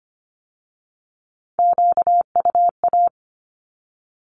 Morse code, heard as a keyed tone, about 25 wpm, 700 Hz. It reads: QUA
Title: Q-Codes with Spoken Meaning - 25wpm